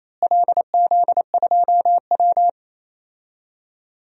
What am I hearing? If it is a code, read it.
Morse code, heard as a keyed tone, about 28 wpm, 700 Hz. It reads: LZ2W
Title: DXCC-100 Call Signs - 4 Characters - Repeated 2x - 28wpm